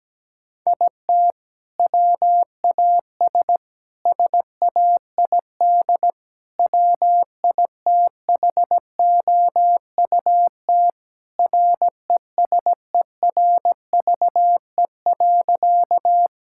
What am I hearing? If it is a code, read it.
Morse code, heard as a keyed tone, about 17 wpm, 700 Hz. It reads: IT WAS SAID WITHOUT RESERVE.
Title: Sentences from Top 1000 Words - Review Part 6 - 17wpm